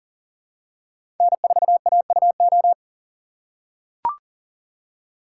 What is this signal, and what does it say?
Morse code, heard as a keyed tone, about 40 wpm, 700 Hz. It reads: N4AUO
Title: USA Call Signs - 5 Characters - 4x Character Spacing - 40wpm